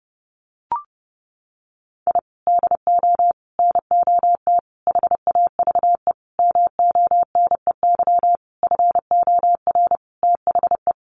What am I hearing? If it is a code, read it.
Morse code, heard as a keyed tone, about 30 wpm, 700 Hz. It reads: I DO NOT HAVE MONEY FOR THE
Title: Sentences from Top 1400 Words 30wpm